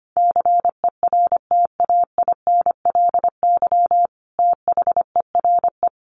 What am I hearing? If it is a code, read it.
Morse code, heard as a keyed tone, about 25 wpm, 700 Hz. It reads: CERTAINLY THERE
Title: Sentences from Top 900 Words 25wpm